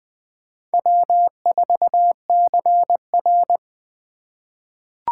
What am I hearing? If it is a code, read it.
Morse code, heard as a keyed tone, about 20 wpm, 700 Hz. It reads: W4CR
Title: USA Call Signs - 4 Characters - 2x Character Spacing - 20wpm